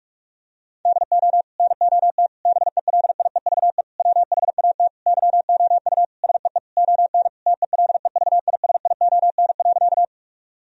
Morse code, heard as a keyed tone, about 47 wpm, 700 Hz. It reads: DO NOT BELIEVE WHAT YOU SEE ON TELEVISION.
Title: Sentences from Top 1000 Words - Review Part 4 - 45wpm